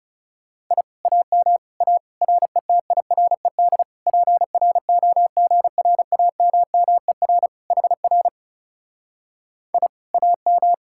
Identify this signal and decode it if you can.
Morse code, heard as a keyed tone, about 37 wpm, 700 Hz. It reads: I AM A RETIRED PROGRAMMER HR I AM
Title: Speed-Racing Common QSO Phrases - 30wpm